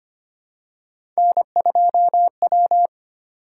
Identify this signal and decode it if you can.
Morse code, heard as a keyed tone, about 25 wpm, 700 Hz. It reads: N2W
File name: Three Letter-Number Combinations - 25wpm